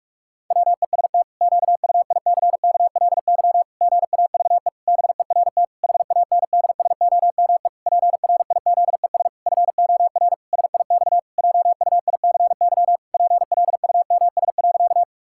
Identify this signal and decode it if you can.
Morse code, heard as a keyed tone, about 46 wpm, 700 Hz. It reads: WEST QUICKLY GAVE BERT HANDSOME PRIZES FOR SIX JUICY PLUMS.
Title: Pangram Challenge v2 - 45wpm